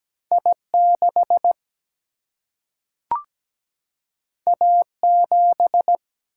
Morse code, heard as a keyed tone, about 17 wpm, 700 Hz. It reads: I6 A7
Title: Single-Letter Single-Number 17wpm